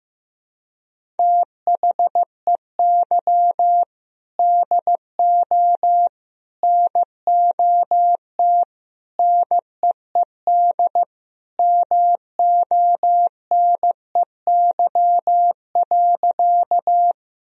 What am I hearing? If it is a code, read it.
Morse code, heard as a keyed tone, about 16 wpm, 700 Hz. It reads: THEY DO NOT NEED MONEY.
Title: Sentences from Top 1000 Words - Review Part 1 - 15wpm